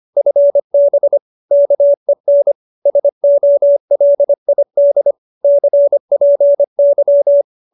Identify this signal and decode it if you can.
Morse code, heard as a keyed tone, about 25 wpm, 570 Hz. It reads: FB KEN SOLID CPY